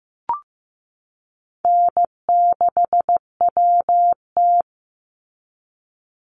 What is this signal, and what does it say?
Morse code, heard as a keyed tone, about 15 wpm, 700 Hz. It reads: N6WT
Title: NA SOTA Chaser Callsigns 15wpm